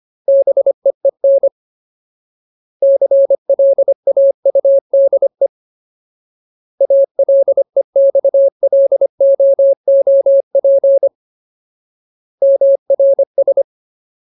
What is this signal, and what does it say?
Morse code, heard as a keyed tone, about 25 wpm, 550 Hz. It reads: BEEN CLAUDE ALEXLOOP MRS
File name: Ultimate Word List / 25WPM Ultimate